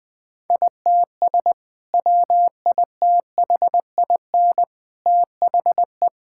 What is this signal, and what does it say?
Morse code, heard as a keyed tone, about 21 wpm, 700 Hz. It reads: ITS WITHIN THE
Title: Sentences with 2-Word Phrases - 20wpm